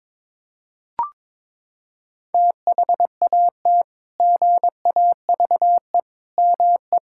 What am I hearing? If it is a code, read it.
Morse code, heard as a keyed tone, about 22 wpm, 700 Hz. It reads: THAT GAVE ME
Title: Sentences from Top 1600 Words 22wpm